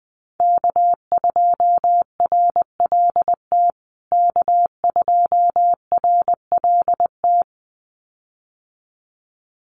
Morse code, heard as a keyed tone, about 20 wpm, 700 Hz. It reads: K2RLT K2RLT
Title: USA Call Signs Repeated 2x 20wpm